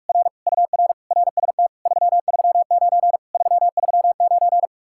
Morse code, heard as a keyed tone, about 49 wpm, 700 Hz. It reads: R UR RST 339 339